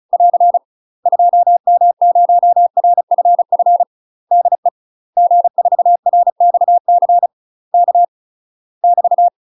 Morse code, heard as a keyed tone, about 36 wpm, 700 Hz. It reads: <AR> 2M0RFF DE G4RXC K <BT>
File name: UK MASTER QSOs TRAINING / 35WPM / UK MASTER QSOs 0029